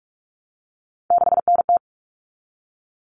Morse code, heard as a keyed tone, about 44 wpm, 700 Hz. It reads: 6NT
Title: All Cut Numbers - 3-Digit 45wpm